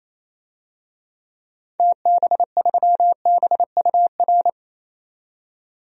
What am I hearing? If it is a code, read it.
Morse code, heard as a keyed tone, about 28 wpm, 700 Hz. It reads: TB3BUR
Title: DXCC-100 Call Signs - 6 Characters - 28wpm